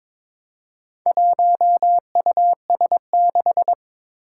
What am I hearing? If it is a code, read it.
Morse code, heard as a keyed tone, about 22 wpm, 700 Hz. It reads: 1US6
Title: Four Challenging Letter-Number Combinations 22wpm